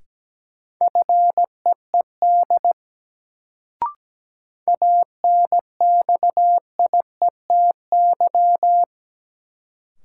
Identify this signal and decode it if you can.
Morse code, heard as a keyed tone, about 17 wpm, 700 Hz. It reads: FEED ANXIETY
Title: Sentences from Top 1600 Words - New Word Drill Alt - 17wpm